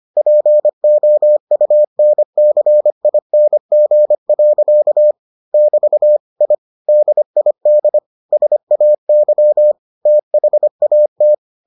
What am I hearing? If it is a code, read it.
Morse code, heard as a keyed tone, about 25 wpm, 600 Hz. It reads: POUNCING. <BT> I DID SAY THAT